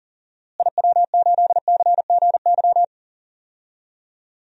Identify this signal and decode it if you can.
Morse code, heard as a keyed tone, about 37 wpm, 700 Hz. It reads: IW8CGY
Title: DXCC-100 Call Signs - 6 Characters - Repeated 3x - 40wpm